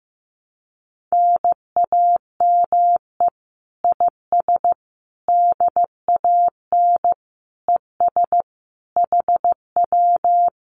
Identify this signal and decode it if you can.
Morse code, heard as a keyed tone, about 15 wpm, 700 Hz. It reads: NAME IS DAN ES HW
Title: Common QSO Phrases Encore 15wpm